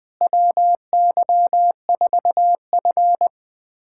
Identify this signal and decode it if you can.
Morse code, heard as a keyed tone, about 20 wpm, 700 Hz. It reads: WY4F
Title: USA Callsign DE Callsign - 4 Characters - 20wpm